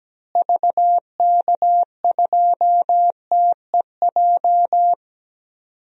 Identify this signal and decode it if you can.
Morse code, heard as a keyed tone, about 17 wpm, 700 Hz. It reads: VK2TEJ
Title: DXCC-100 Call Signs - 6 Characters - Repeated 2x - 17wpm